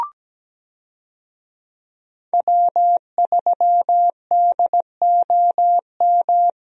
Morse code, heard as a keyed tone, about 17 wpm, 700 Hz. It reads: W3DOM
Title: USA Call Signs - 5 Characters - Repeated 2x - 17wpm